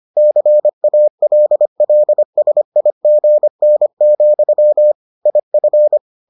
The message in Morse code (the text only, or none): CALLSIGN, IF